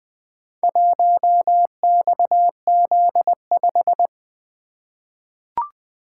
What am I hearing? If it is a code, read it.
Morse code, heard as a keyed tone, about 20 wpm, 700 Hz. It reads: 1XZ5
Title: Four Letter-Number Combinations - 20wpm